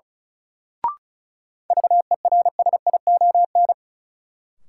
Morse code, heard as a keyed tone, about 36 wpm, 700 Hz. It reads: VERSION
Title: Sentences from Top 1400 Words - New Word Drill - 35wpm